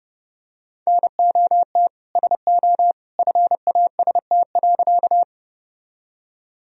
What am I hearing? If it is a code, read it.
Morse code, heard as a keyed tone, about 32 wpm, 700 Hz. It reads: NOT SO FAST.
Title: Sentences from Top 1000 Words - Review Part 2 - 30wpm